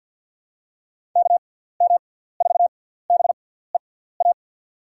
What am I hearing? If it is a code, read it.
Morse code, heard as a keyed tone, about 53 wpm, spaced as for 17 wpm, 700 Hz. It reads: KM4BEA
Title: USA Call Signs - 6 Characters - 6x Character Spacing - 50wpm